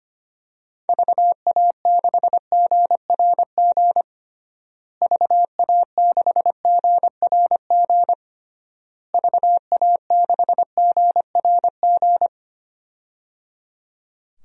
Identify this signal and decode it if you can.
Morse code, heard as a keyed tone, about 25 wpm, 700 Hz. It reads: VA6GRG VA6GRG VA6GRG
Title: DXCC-100 Call Signs - 6 Characters - Repeated 3x - 25wpm